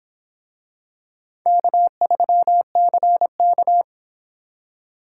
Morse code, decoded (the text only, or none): K3CK